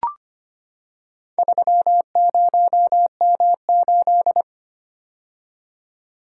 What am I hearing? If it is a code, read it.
Morse code, heard as a keyed tone, about 25 wpm, 690 Hz. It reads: 30M8
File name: practice / 25wpm / Four Letter-Number Combinations - 25wpm